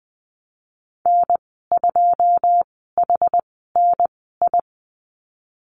Morse code, heard as a keyed tone, about 20 wpm, spaced as for 15 wpm, 700 Hz. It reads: N2HNI
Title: 122 Lesson 20g - Number 3 - Call Signs 20wpm